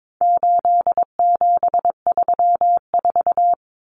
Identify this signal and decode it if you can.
Morse code, heard as a keyed tone, about 22 wpm, 700 Hz. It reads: 8734
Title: Common Cut Numbers - 4-Digit 22wpm